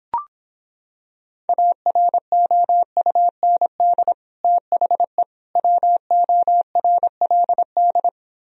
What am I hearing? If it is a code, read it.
Morse code, heard as a keyed tone, about 26 wpm, 700 Hz. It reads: AROUND THE WORLD